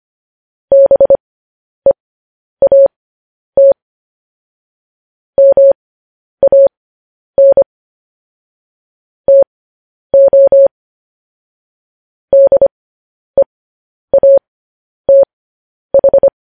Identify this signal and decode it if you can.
Morse code, heard as a keyed tone, about 25 wpm, spaced as for 10 wpm, 550 Hz. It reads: BEAT MAN TO DEATH
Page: news-headlines-in-morse-code-at-10-wpm